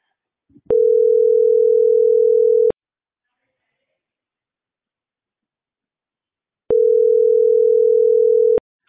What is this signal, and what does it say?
Ringback tone (North America), 440 + 480 Hz.